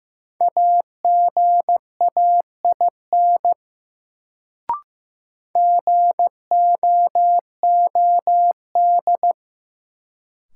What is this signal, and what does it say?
Morse code, heard as a keyed tone, about 15 wpm, 700 Hz. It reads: AGAIN GOOD
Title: Sentences from Top 200 Words - New Word Drill - 15wpm